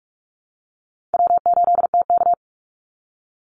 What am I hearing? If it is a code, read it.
Morse code, heard as a keyed tone, about 44 wpm, 700 Hz. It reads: W8TK
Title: Easy Call Signs Repeated 3x - 45wpm